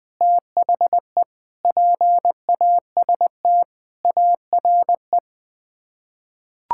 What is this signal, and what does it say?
Morse code, heard as a keyed tone, about 20 wpm, 700 Hz. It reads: THE PAST ARE
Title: Sets of 8 Words - 20wpm